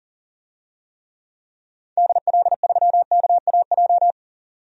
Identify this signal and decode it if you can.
Morse code, heard as a keyed tone, about 40 wpm, 700 Hz. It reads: DL3KAJ
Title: DXCC-100 Call Signs - 6 Characters - 40wpm